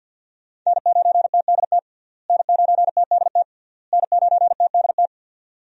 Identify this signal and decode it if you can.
Morse code, heard as a keyed tone, about 50 wpm, 700 Hz. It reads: N9TDT N9TDT N9TDT
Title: USA Call Signs - 5 Characters - Repeated 3x - 50wpm